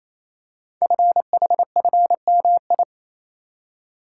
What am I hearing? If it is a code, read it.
Morse code, heard as a keyed tone, about 28 wpm, 700 Hz. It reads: FHFMI
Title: Code Groups - 5 Letter:Numbers - 28wpm